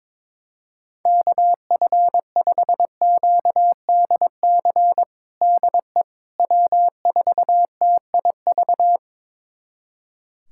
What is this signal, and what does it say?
Morse code, heard as a keyed tone, about 21 wpm, 700 Hz. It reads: KF5QDC DE W4TIV
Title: USA Callsign DE Callsign 22wpm